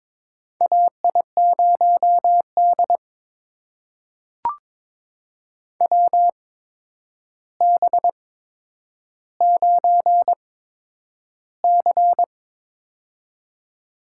Morse code, heard as a keyed tone, about 21 wpm, spaced as for 9 wpm, 700 Hz. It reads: AI0D WB9C
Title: USA Call Signs - 4 Characters - 8x Character Spacing - 22wpm